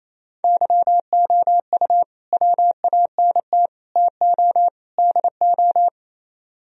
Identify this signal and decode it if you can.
Morse code, heard as a keyed tone, about 34 wpm, 700 Hz. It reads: YOU WANT TO DO
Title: Speed-Racing - Sets of 4 Words 28wpm